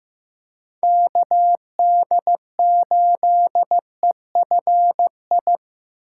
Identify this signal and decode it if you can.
Morse code, heard as a keyed tone, about 15 wpm, 700 Hz. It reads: KD8EFI
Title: USA Callsign DE Callsign - 6 Characters - 15wpm